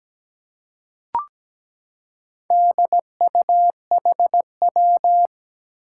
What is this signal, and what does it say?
Morse code, heard as a keyed tone, about 17 wpm, 700 Hz. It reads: DUHW
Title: Four Challenging Letter-Number Combinations 17wpm